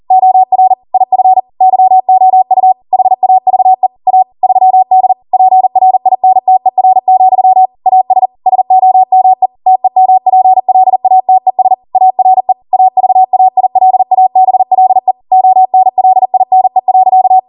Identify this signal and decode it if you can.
Morse code, heard as a keyed tone, about 40 wpm, 750 Hz. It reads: OR IF YOU HAVE A 3D PRINTER, AS SOME TEMPLATES ARE AVAILABLE ONLINE.